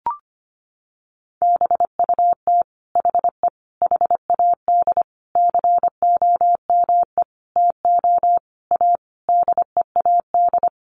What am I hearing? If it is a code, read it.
Morse code, heard as a keyed tone, about 25 wpm, 700 Hz. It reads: BUT HE HAD COME TO A DEAD
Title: Sentences from Top 1500 Words 25wpm